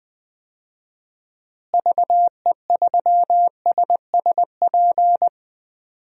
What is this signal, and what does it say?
Morse code, heard as a keyed tone, about 19 wpm, 700 Hz. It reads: VE3SSP
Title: DXCC-100 Call Signs - 6 Characters - Repeated 3x - 20wpm